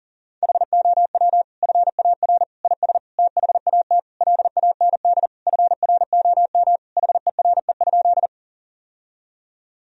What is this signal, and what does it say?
Morse code, heard as a keyed tone, about 40 wpm, 700 Hz. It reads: HOW FAR IS THAT LAND FROM HERE?
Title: Sentences from Top 1000 Words - Review Part 1 - 40wpm